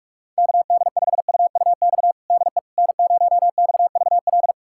Morse code, heard as a keyed tone, about 45 wpm, 700 Hz. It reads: KD5UUX DE N0XVL
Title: USA Callsign DE Callsign 45wpm